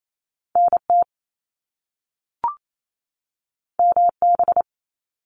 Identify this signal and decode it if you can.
Morse code, heard as a keyed tone, about 28 wpm, 700 Hz. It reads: NT MB
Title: Canadian Provinces and Territories 28wpm